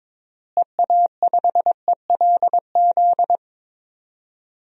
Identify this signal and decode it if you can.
Morse code, heard as a keyed tone, about 21 wpm, 700 Hz. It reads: EA5ELZ
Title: DXCC-100 Call Signs - 6 Characters - Repeated 3x - 22wpm